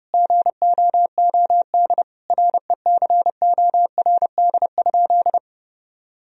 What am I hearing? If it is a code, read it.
Morse code, heard as a keyed tone, about 30 wpm, 700 Hz. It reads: GOOD RECORD?
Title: Sentences from Top 1000 Words - Review Part 3 - 30wpm